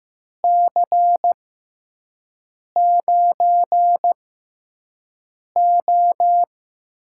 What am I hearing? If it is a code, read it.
Morse code, heard as a keyed tone, about 15 wpm, spaced as for 5 wpm, 700 Hz. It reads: C9O
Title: USA Call Signs - 6x Character Spacing - 15wpm